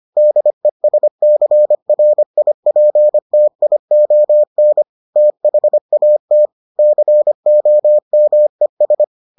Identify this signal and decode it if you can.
Morse code, heard as a keyed tone, about 25 wpm, 600 Hz. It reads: DESCRIPTION THAT COMES